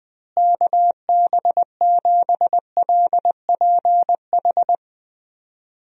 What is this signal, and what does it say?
Morse code, heard as a keyed tone, about 21 wpm, 700 Hz. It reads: KB7LPH
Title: USA Call Signs - 6 Characters - Repeated 3x - 20wpm